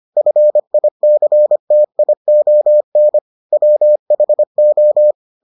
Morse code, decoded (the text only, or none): FICTION WHO